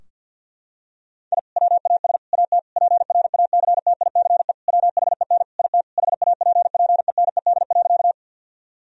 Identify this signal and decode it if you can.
Morse code, heard as a keyed tone, about 51 wpm, 700 Hz. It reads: I WAS AT PRACTICE WHEN IT HAPPENED.
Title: Sentences from Top 1000 Words - Review Part 4 - 50wpm